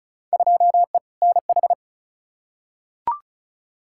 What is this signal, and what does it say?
Morse code, heard as a keyed tone, about 36 wpm, 700 Hz. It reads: 2E NH
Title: Field Day Exchanges - 35wpm